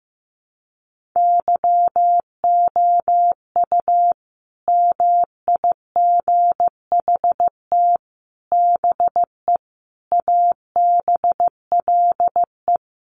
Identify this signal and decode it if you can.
Morse code, heard as a keyed tone, about 15 wpm, 700 Hz. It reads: YOU MIGHT BE ABLE
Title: Sets of 5 Words Encore2 15wpm